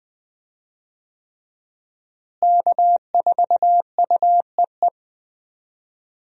Morse code, heard as a keyed tone, about 20 wpm, 700 Hz. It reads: K4UEE
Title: USA Call Signs - 5 Characters - Repeated 3x - 20wpm